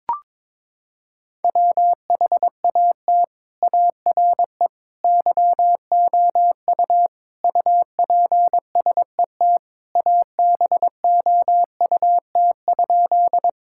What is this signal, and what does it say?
Morse code, heard as a keyed tone, about 22 wpm, 700 Hz. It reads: WHAT ARE YOU UPSET ABOUT?
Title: Sentences from Top 1000 Words - Review Part 5 - 22wpm